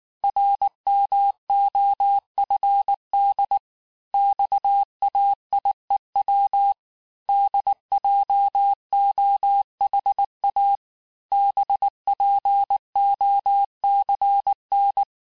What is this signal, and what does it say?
Morse code, heard as a keyed tone, about 19 wpm, 800 Hz. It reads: RMOFD XAIEW DJOHA BPOCN
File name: cours-du-lundi files / cours-1595-teste-CHIFFRE-vitesse18-MPM